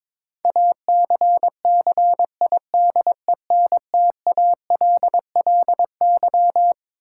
Morse code, heard as a keyed tone, about 22 wpm, 700 Hz. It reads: ACCIDENTALLY
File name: Twelve Letter Words - 22wpm